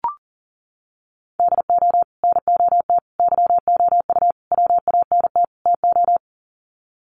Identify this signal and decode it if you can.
Morse code, heard as a keyed tone, about 40 wpm, 700 Hz. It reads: DO NOT YOU WANT TO
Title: Sets of 5 Words Encore2 40wpm